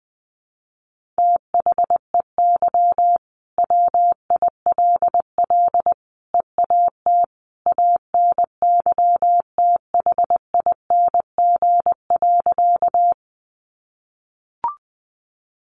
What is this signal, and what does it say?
Morse code, heard as a keyed tone, about 20 wpm, 700 Hz. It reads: THEY WILL EAT ANYTHING.
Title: Sentences from Top 1000 Words 20wpm